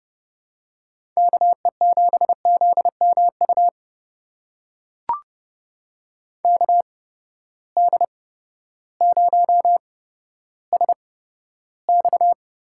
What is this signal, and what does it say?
Morse code, heard as a keyed tone, about 30 wpm, spaced as for 14 wpm, 700 Hz. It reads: KE7ZMU KD0SX
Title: USA Call Signs - 6 Characters - 8x Character Spacing - 30wpm